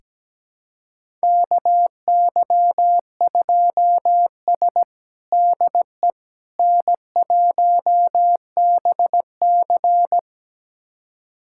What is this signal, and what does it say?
Morse code, heard as a keyed tone, about 17 wpm, 700 Hz. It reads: KY2S DE N1BC
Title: USA Callsign DE Callsign - 4 Characters - 17wpm